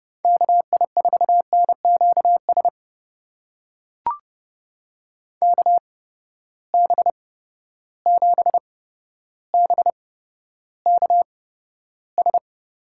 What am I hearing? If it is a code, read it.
Morse code, heard as a keyed tone, about 32 wpm, spaced as for 13 wpm, 700 Hz. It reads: KI4NQS KB7BKS
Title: USA Call Signs - 6 Characters - 8x Character Spacing - 30wpm